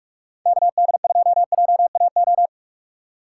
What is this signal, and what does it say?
Morse code, heard as a keyed tone, about 46 wpm, 700 Hz. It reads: KD2JAO
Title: Speed Racing - USA 2x3 Call Signs 45wpm